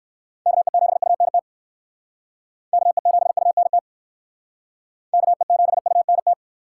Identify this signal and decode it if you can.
Morse code, heard as a keyed tone, about 59 wpm, 700 Hz. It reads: KE7UNT KE7UNT KE7UNT